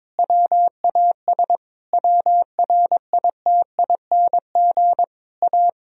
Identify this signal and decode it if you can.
Morse code, heard as a keyed tone, about 22 wpm, 700 Hz. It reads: WAS WRITING A
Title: Sentences from Top 1000 Words - Review Part 4 - 22wpm